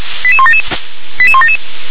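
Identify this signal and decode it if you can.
Five-tone selective call (ZVEI), heard as quick runs of five tones, tones of about 70 ms. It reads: 88177 88177
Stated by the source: ZVEI five-tone selective call received by SDR.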